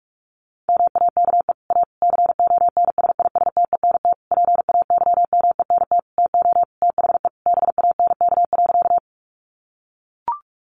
Morse code, heard as a keyed tone, about 44 wpm, 700 Hz. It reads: MAKE A CONSISTENT PAYMENT TO THE BANK.
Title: Sentences from Top 1500 Words 45wpm